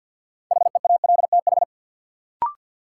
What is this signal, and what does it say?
Morse code, heard as a keyed tone, about 48 wpm, 700 Hz. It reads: HEALTH
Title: Sets of 5 Words - New Word Drill - 50wpm